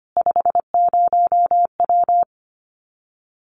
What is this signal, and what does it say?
Morse code, heard as a keyed tone, about 25 wpm, 700 Hz. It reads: 50W
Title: Common QSO Phrases Encore 25wpm